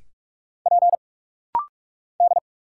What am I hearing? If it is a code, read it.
Morse code, heard as a keyed tone, about 44 wpm, 700 Hz. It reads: P D
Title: Single Letters - Rapid-Fire 45wpm